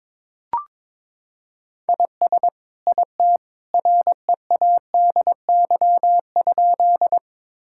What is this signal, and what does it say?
Morse code, heard as a keyed tone, about 22 wpm, 700 Hz. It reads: IS IT READY?
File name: practice / 22wpm / Sentences with 2-Word Phrases - 22wpm